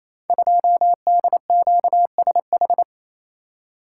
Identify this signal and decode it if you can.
Morse code, heard as a keyed tone, about 27 wpm, 700 Hz. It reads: 2DQSH
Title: Code Groups - 5 Letter:Numbers - 28wpm